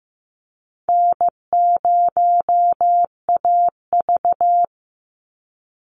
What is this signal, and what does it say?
Morse code, heard as a keyed tone, about 15 wpm, 700 Hz. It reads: N0AV
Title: Easy Call Signs Repeated 3x - 15wpm